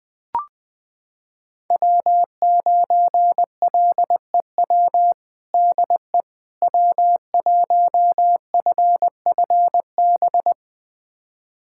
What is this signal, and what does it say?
Morse code, heard as a keyed tone, about 20 wpm, 700 Hz. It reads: W9LEW DE W1FFB
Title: USA Callsign DE Callsign - 5 Characters - 20wpm